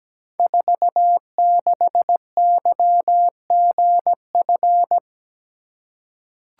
Morse code, heard as a keyed tone, about 17 wpm, 700 Hz. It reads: 46YGF
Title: Code Groups - 5 Letter:Numbers - 17wpm